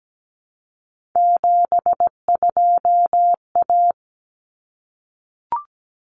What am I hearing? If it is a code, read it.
Morse code, heard as a keyed tone, about 17 wpm, 700 Hz. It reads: 72A
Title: Common Cut Numbers - 3-Digit 17wpm